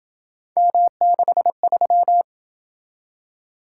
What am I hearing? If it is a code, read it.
Morse code, heard as a keyed tone, about 27 wpm, 700 Hz. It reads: M63